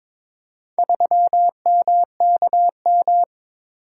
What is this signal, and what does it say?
Morse code, heard as a keyed tone, about 22 wpm, 700 Hz. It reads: 3MKM
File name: practice / 22wpm / Four Letter-Number Combinations - 22wpm